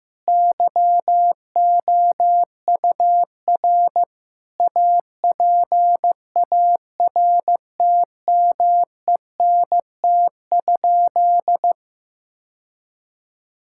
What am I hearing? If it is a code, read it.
Morse code, heard as a keyed tone, about 15 wpm, 700 Hz. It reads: YOUR APARTMENT?
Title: Sentences from Top 1000 Words - Review Part 6 - 15wpm